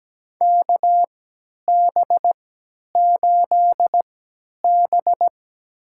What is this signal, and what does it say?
Morse code, heard as a keyed tone, about 17 wpm, spaced as for 10 wpm, 700 Hz. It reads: KB8B
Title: USA Call Signs - 4 Characters - 3x Character Spacing - 17wpm